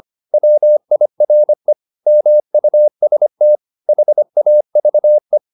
Morse code, heard as a keyed tone, about 25 wpm, 600 Hz. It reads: WIRE MUST HAVE
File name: the old man and the sea part 1 25wpm